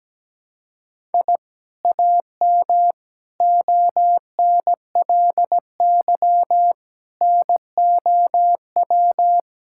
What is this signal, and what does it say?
Morse code, heard as a keyed tone, about 17 wpm, 700 Hz. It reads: I AM ONLY NOW
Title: Sentences from Top 1000 Words - Review Part 5 - 17wpm